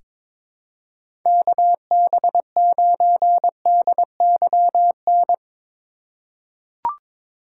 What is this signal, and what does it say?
Morse code, heard as a keyed tone, about 21 wpm, 700 Hz. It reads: KB9DYN
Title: USA Call Signs - 6 Characters - 4x Character Spacing - 22wpm